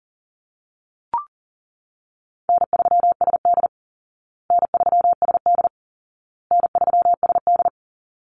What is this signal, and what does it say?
Morse code, heard as a keyed tone, about 37 wpm, 700 Hz. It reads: N3SD N3SD N3SD
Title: Easy Call Signs Repeated 3x - 40wpm